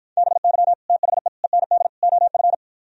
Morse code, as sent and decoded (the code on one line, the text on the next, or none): -... -.-- / - .... / . -. -.. / --- ..-.
BY THE END OF